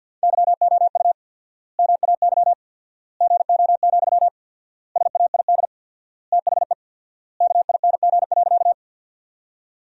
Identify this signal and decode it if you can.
Morse code, heard as a keyed tone, about 50 wpm, 700 Hz. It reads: YOU MAY GO, SAID THE KING.
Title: 4x Word-Spacing - Sentences from Top 400 Words - 50wpm